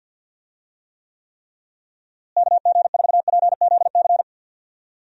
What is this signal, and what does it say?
Morse code, heard as a keyed tone, about 50 wpm, 700 Hz. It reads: KG4PZC
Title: USA Call Signs - 6 Characters - Repeated 3x - 50wpm